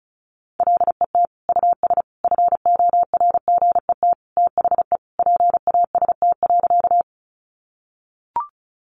Morse code, heard as a keyed tone, about 34 wpm, 700 Hz. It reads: LET US FORGET THE PAST.
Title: Sentences from Top 1200 Words 35wpm